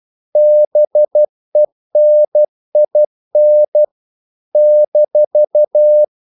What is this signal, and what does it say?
Morse code, heard as a keyed tone, about 12 wpm, 600 Hz. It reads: BENIN -
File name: CW / Practice / NationalCapitals12